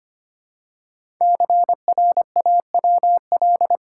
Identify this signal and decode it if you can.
Morse code, heard as a keyed tone, about 25 wpm, 700 Hz. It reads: CRAWL
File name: 5 Letter Words - First 3 Letters Given - 25wpm